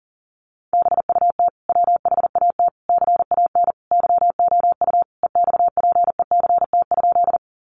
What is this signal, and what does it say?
Morse code, heard as a keyed tone, about 40 wpm, 700 Hz. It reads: BUT WHAT CAN YOU EXPECT?
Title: Sentences from Top 1100 Words 40wpm